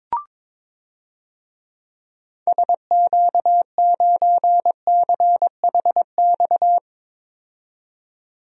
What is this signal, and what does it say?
Morse code, heard as a keyed tone, about 22 wpm, 700 Hz. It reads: SQ9CHX
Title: DXCC-100 Call Signs - 6 Characters - 22wpm